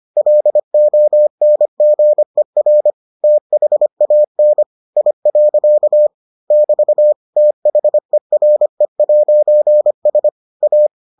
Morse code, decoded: LONGER THAN I. <BT> THERE'S A

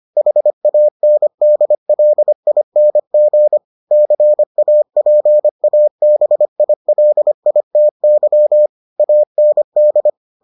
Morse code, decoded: HANDLING CAPABILITY AND

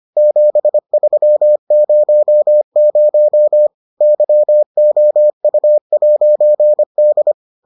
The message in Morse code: 7300 YOU'D

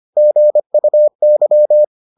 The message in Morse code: GUY

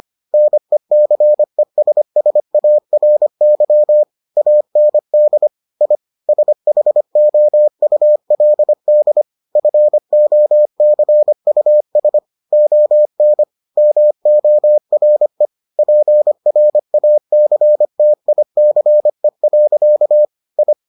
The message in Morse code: NECESSARY AND I SHOULD FOCUS ON MORE PRACTICE. I

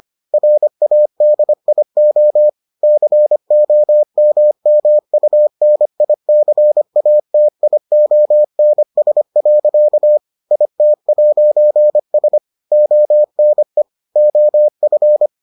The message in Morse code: RADIO COMMUNICATIONS. IT'S ONE OF

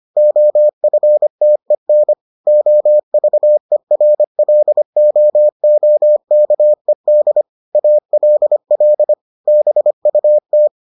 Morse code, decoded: OFTEN OVERLOOKED ALL BUT

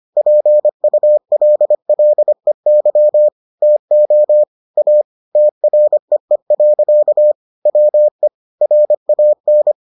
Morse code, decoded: PULLEY TO A TREE. WE RAN

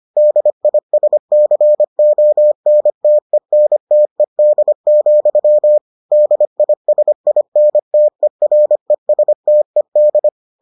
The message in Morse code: DISCONTENTED, DISINTERESTED

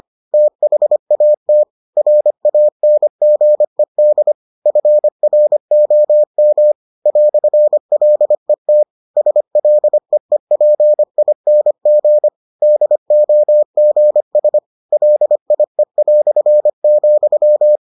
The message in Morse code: THAT RANGED FROM "LET SLEEPING DOGS LIE",

